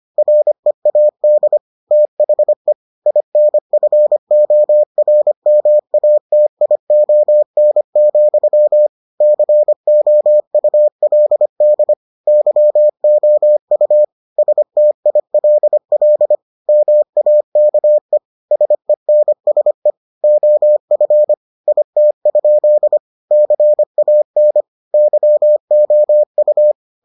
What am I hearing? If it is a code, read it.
Morse code, heard as a keyed tone, about 25 wpm, 600 Hz. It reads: READ THE INFORMATION, COULD YOU STILL MAKE SENSE OF IT? CAN YOU